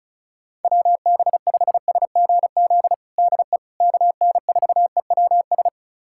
Morse code, heard as a keyed tone, about 36 wpm, 690 Hz. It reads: WB5SGZ DE KN4EWS
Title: USA Callsign DE Callsign 35wpm